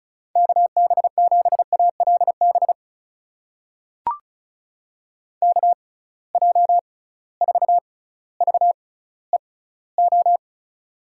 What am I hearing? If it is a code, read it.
Morse code, heard as a keyed tone, about 36 wpm, spaced as for 18 wpm, 700 Hz. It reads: KB7ALB KJ4VEO
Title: USA Call Signs - 6 Characters - 6x Character Spacing - 35wpm